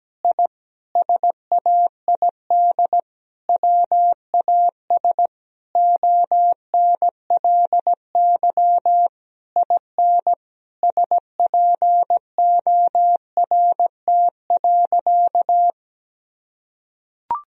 Morse code, heard as a keyed tone, about 17 wpm, 700 Hz. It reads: I SAID WAS ONLY IN SPORT.
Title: Sentences from Top 1000 Words - Review Part 4 - 17wpm